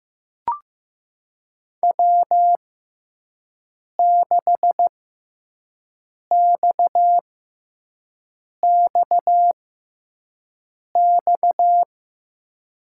Morse code, heard as a keyed tone, about 15 wpm, spaced as for 5 wpm, 700 Hz. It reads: W6XXX
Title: USA Call Signs - 6x Character Spacing - 15wpm